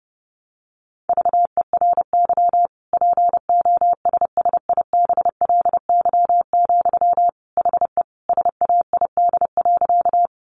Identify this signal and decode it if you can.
Morse code, heard as a keyed tone, about 30 wpm, 700 Hz. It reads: VERY POSSIBLY, HE SAID.
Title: Sentences from Top 1300 Words 30wpm